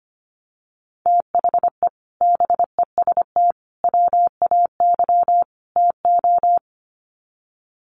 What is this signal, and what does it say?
Morse code, heard as a keyed tone, about 25 wpm, 700 Hz. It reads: THE BEST WAY TO
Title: Sets of 4 Words 25wpm